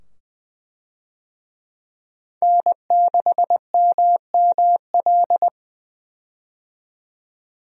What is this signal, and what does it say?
Morse code, heard as a keyed tone, about 20 wpm, 700 Hz. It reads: N6MML
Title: USA Call Signs - 5 Characters - 20wpm